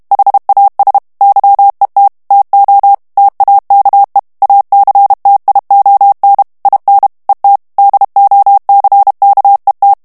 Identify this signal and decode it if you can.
Morse code, heard as a keyed tone, about 32 wpm, 800 Hz. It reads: HAS YET TO TAKE ACTION IN ET DOCKET
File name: morse mp3 / ANGLAIS texte clair 3 vitesse 30 mots minute